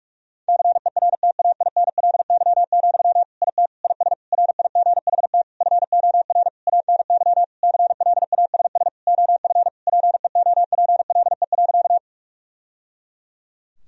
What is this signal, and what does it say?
Morse code, heard as a keyed tone, about 41 wpm, 700 Hz. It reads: CERTAINLY, IT IS RIGHT FOR ANY CLASS OF PEOPLE.
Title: Sentences from Top 1000 Words - Review Part 2 - 45wpm